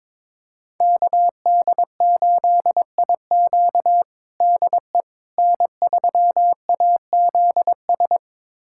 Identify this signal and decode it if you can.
Morse code, heard as a keyed tone, about 22 wpm, 700 Hz. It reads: KD8IQ DE N3AZS
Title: USA Callsign DE Callsign - 5 Characters - 22wpm